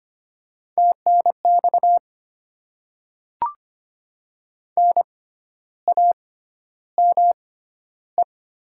Morse code, heard as a keyed tone, about 25 wpm, spaced as for 12 wpm, 700 Hz. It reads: TNX NAME
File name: Top 20 QSO Elements - 6x Character Spacing - 25wpm